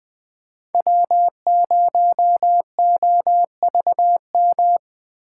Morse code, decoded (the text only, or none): W0OVM